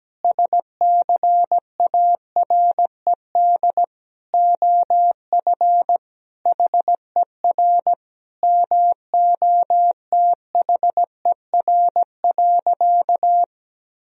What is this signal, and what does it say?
Morse code, heard as a keyed tone, about 17 wpm, 700 Hz. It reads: SCARED OF HER MOTHER.
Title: Sentences from Top 1000 Words - Review Part 4 - 17wpm